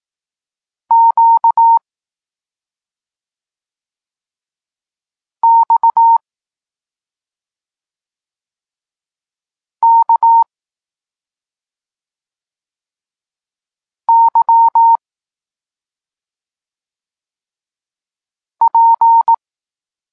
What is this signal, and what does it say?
Morse code, heard as a keyed tone, about 18 wpm, 920 Hz. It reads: Q X K Y P
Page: Tekst 32 – Doskonalenie odbioru poznanych liter